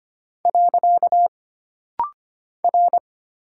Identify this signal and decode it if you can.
Morse code, heard as a keyed tone, about 25 wpm, 700 Hz. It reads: . R
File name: Single Character (Letter:Number:Punctuation:Prosign) - Rapid-Fire - 25wpm